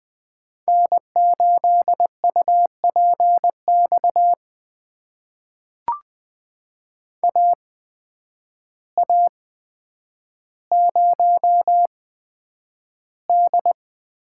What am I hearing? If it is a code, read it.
Morse code, heard as a keyed tone, about 20 wpm, spaced as for 9 wpm, 700 Hz. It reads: N8UPX AA0D
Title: USA Call Signs - 5 Characters - 8x Character Spacing - 20wpm